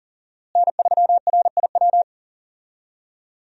Morse code, heard as a keyed tone, about 40 wpm, 700 Hz. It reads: N3RIW
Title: USA Call Signs - 5 Characters - Repeated 3x - 40wpm